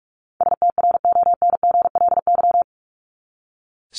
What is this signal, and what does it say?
Morse code, heard as a keyed tone, about 44 wpm, 700 Hz. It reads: STRONGLY
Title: LY Suffix - Top 100 Words with Base Words 45wpm